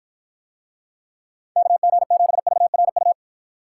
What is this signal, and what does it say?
Morse code, heard as a keyed tone, about 53 wpm, 700 Hz. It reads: KG7VRU